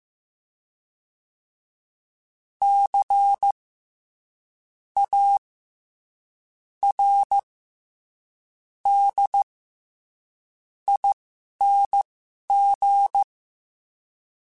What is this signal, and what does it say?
Morse code, heard as a keyed tone, about 15 wpm, spaced as for 6 wpm, 800 Hz. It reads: CARDING
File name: Special INGs-mp3s / Special INGs-mp3s / SlowToFaster ing @10